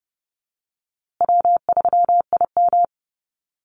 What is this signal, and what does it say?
Morse code, heard as a keyed tone, about 30 wpm, 700 Hz. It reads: W3IM
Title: USA Call Signs Repeated 2x 30wpm